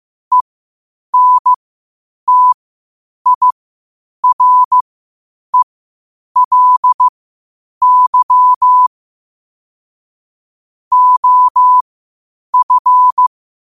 Morse code, heard as a keyed tone, about 15 wpm, spaced as for 8 wpm, 1000 Hz. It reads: ENTIRELY OF
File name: qotdfiles / 8wpm